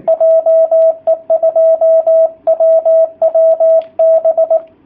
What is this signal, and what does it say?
Morse code, heard as a keyed tone, about 19 wpm, 660 Hz, with radio noise. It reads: JE2WWB
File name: jh8cbh / 2019cw / sp18- w